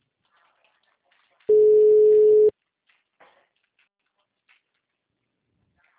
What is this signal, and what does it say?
Ringback tone (Europe, 425 Hz).